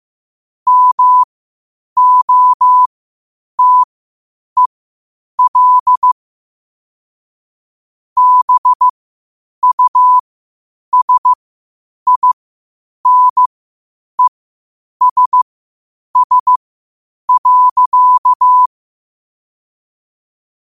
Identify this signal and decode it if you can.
Morse code, heard as a keyed tone, about 15 wpm, spaced as for 8 wpm, 1000 Hz. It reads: MOTEL BUSINESS.
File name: qotdfiles / 8wpm